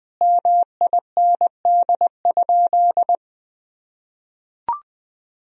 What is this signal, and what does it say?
Morse code, heard as a keyed tone, about 20 wpm, 700 Hz. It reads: MIND?
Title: Sentences from Top 1000 Words - Review Part 2 - 20wpm